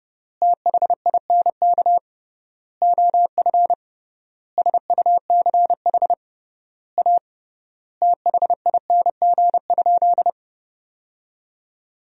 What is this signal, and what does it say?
Morse code, heard as a keyed tone, about 30 wpm, 700 Hz. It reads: THINK OF SUCH A THING?
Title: 3x Word-Spacing - Sentences from Top 200 Words - 30wpm